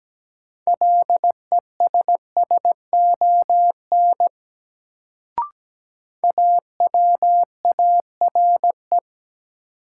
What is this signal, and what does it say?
Morse code, heard as a keyed tone, about 17 wpm, 700 Hz. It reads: LESSON AWARE
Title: Sentences from Top 1200 Words - New Word Drill Alt - 17wpm